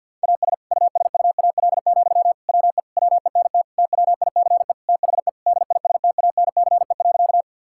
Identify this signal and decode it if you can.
Morse code, heard as a keyed tone, about 50 wpm, 700 Hz. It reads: AS USUAL, WE WENT TWICE THE DISTANCE.
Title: Sentences from Top 1600 Words 50wpm